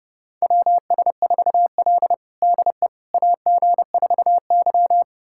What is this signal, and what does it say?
Morse code, heard as a keyed tone, about 30 wpm, 700 Hz. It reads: WS4L DE AG4Y
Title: USA Callsign DE Callsign - 4 Characters - 30wpm